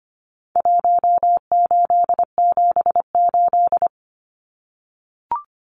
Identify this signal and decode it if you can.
Morse code, heard as a keyed tone, about 25 wpm, 700 Hz. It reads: 1878
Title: Numbers - Contest Style - 4-Digit 25wpm